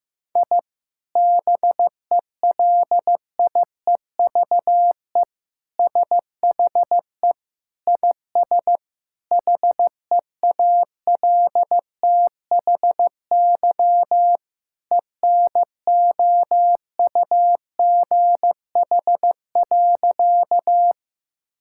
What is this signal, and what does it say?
Morse code, heard as a keyed tone, about 15 wpm, 700 Hz. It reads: I BELIEVE SHE IS HEALTHY ENOUGH.
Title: Sentences from Top 1000 Words - Review Part 5 - 15wpm